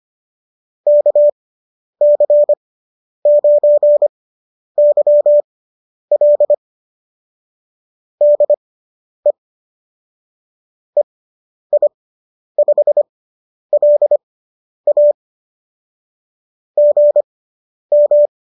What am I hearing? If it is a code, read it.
Morse code, heard as a keyed tone, about 25 wpm, spaced as for 10 wpm, 590 Hz. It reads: KC9YL DE EI5LA GM